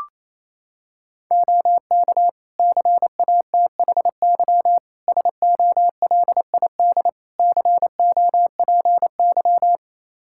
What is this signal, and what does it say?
Morse code, heard as a keyed tone, about 28 wpm, 700 Hz. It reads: OK CATHY SOLID COPY
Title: K1USN SST QSO Practice 28wpm